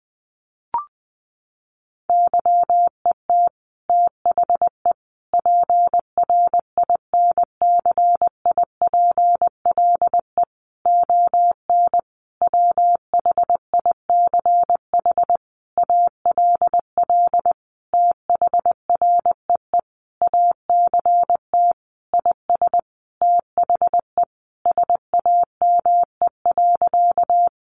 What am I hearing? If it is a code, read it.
Morse code, heard as a keyed tone, about 19 wpm, 700 Hz. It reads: YET THE PRINCIPLE ON WHICH ALL THREE ACT IS THE SAME.
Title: Sentences from Top 1400 Words 20wpm